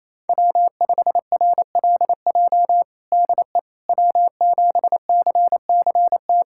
Morse code, heard as a keyed tone, about 27 wpm, 700 Hz. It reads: W5RLJ DE W7CCT
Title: USA Callsign DE Callsign - 5 Characters - 28wpm